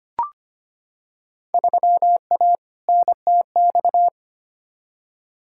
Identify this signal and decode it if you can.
Morse code, heard as a keyed tone, about 25 wpm, 700 Hz. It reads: 3A NTX
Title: Field Day Exchanges - 25wpm